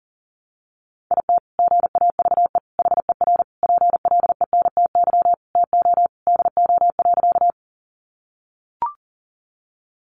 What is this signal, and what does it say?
Morse code, heard as a keyed tone, about 40 wpm, 700 Hz. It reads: IT GAVE HER PLENTY TO DO.
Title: Sentences from Top 1300 Words 40wpm